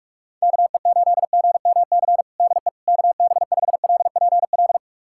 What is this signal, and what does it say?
Morse code, heard as a keyed tone, about 41 wpm, 700 Hz. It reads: KE8GMC DE KB5LPL
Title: USA Callsign DE Callsign - 6 Characters - 45wpm